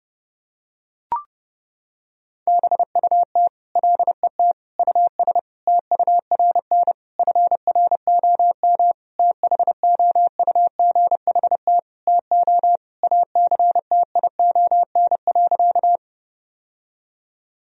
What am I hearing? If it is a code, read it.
Morse code, heard as a keyed tone, about 32 wpm, 700 Hz. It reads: BUT LET US TURN FROM THOUGHT TO ACTION.
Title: Sentences from Top 1000 Words - Review Part 4 - 30wpm